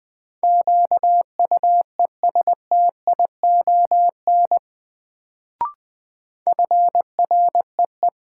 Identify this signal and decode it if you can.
Morse code, heard as a keyed tone, about 21 wpm, 700 Hz. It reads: QUESTION FREE
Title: Sets of 5 Words Encore - New Word Drill - 20wpm